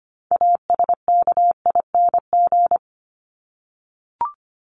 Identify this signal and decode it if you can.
Morse code, heard as a keyed tone, about 25 wpm, 700 Hz. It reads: ASKING
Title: ING Suffix - Top 100 Words with Base Words 25wpm